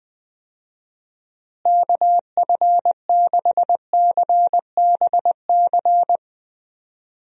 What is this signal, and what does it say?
Morse code, heard as a keyed tone, about 20 wpm, 700 Hz. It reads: KF6CBC
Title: USA Call Signs - 6 Characters - Repeated 2x - 20wpm